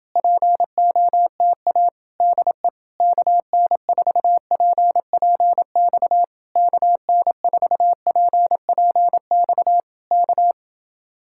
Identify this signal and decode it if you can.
Morse code, heard as a keyed tone, about 27 wpm, 700 Hz. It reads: POTA DE KN4PPX KN4PPX K